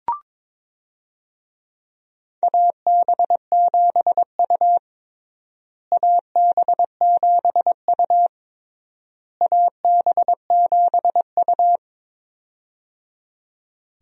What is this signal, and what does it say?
Morse code, heard as a keyed tone, about 22 wpm, 700 Hz. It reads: AB7U AB7U AB7U
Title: USA Call Signs - 4 Characters - Repeated 3x - 22wpm